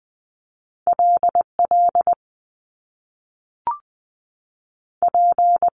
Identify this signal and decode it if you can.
Morse code, heard as a keyed tone, about 20 wpm, 700 Hz. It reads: L L P
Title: Top Two Letter Combinations in Words 20wpm